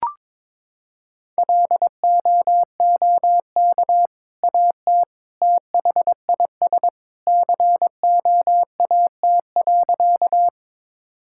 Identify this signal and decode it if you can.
Morse code, heard as a keyed tone, about 22 wpm, 700 Hz. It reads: LOOK AT THIS COAT.
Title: Sentences from Top 1600 Words 22wpm